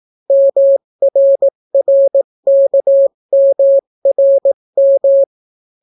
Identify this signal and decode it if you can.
Morse code, heard as a keyed tone, about 18 wpm, 550 Hz. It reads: MRRKMRM